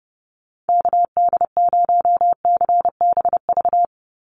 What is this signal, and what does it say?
Morse code, heard as a keyed tone, about 30 wpm, 700 Hz. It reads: KD0CBV
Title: USA Call Signs 30wpm